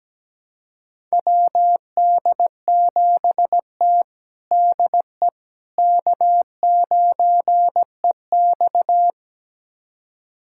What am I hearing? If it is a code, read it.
Morse code, heard as a keyed tone, about 17 wpm, 700 Hz. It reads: WD7T DE K9EX
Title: USA Callsign DE Callsign - 4 Characters - 17wpm